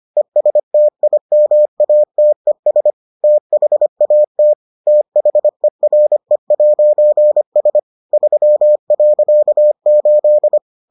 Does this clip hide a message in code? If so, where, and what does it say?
Yes: Morse code from 0.2 to 10.6 s, heard as a keyed tone, about 25 wpm, 600 Hz. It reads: ESTIMATES THAT THERE'S 3.8